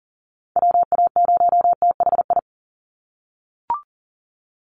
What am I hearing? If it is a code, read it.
Morse code, heard as a keyed tone, about 40 wpm, 700 Hz. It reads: WA0THI
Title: USA Call Signs 40wpm